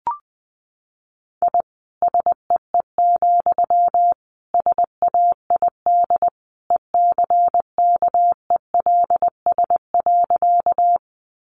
Morse code, heard as a keyed tone, about 20 wpm, 700 Hz. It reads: I SEE, SAID ECKELS.
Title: A Sound Of Thunder 20wpm